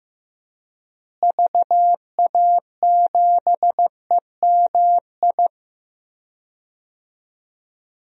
Morse code, heard as a keyed tone, about 15 wpm, 700 Hz. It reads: VA7EMI
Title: DXCC-100 Call Signs - 6 Characters - Repeated 2x - 15wpm